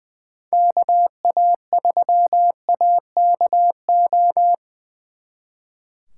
Morse code, heard as a keyed tone, about 24 wpm, 700 Hz. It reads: KA3AKO
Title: Speed Racing - USA 2x3 Call Signs 20wpm